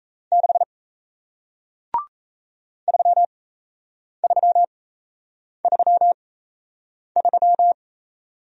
Morse code, heard as a keyed tone, about 38 wpm, spaced as for 8 wpm, 700 Hz. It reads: 6 3333